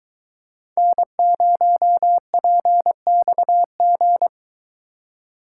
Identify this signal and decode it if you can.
Morse code, heard as a keyed tone, about 21 wpm, 700 Hz. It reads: N0PXG